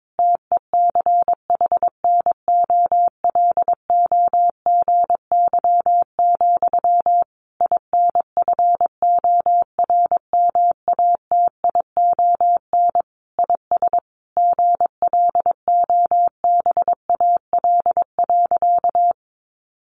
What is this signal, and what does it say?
Morse code, heard as a keyed tone, about 22 wpm, 700 Hz. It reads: TECHNOLOGY, INFORMATION IS GLOBAL.
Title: Sentences from Top 1200 Words 22wpm